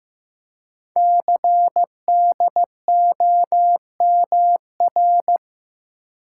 Morse code, heard as a keyed tone, about 15 wpm, 700 Hz. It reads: CDOMR
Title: Code Groups - 5 Letters - 15wpm